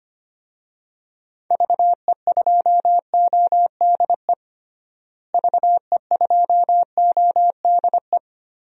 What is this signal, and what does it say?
Morse code, heard as a keyed tone, about 25 wpm, 700 Hz. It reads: VE2ODE VE2ODE
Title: DXCC-100 Call Signs - 6 Characters - Repeated 2x - 25wpm